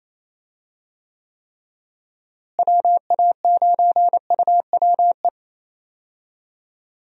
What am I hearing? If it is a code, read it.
Morse code, heard as a keyed tone, about 28 wpm, 700 Hz. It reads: WA9UWE
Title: USA Call Signs - 6 Characters - Repeated 3x - 28wpm